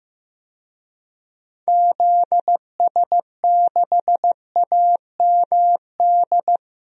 Morse code, heard as a keyed tone, about 15 wpm, 700 Hz. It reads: ZS6AMD
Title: DXCC-100 Call Signs - 6 Characters - Repeated 2x - 15wpm